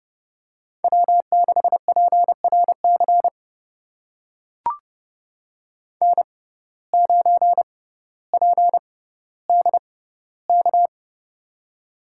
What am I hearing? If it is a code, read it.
Morse code, heard as a keyed tone, about 30 wpm, spaced as for 15 wpm, 700 Hz. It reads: W6PRC N9PDK
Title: USA Call Signs - 5 Characters - 6x Character Spacing - 30wpm